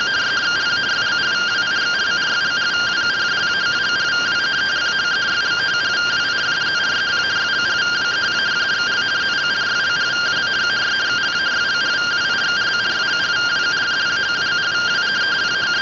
Two-tone teleprinter signal, about 170 Hz shift.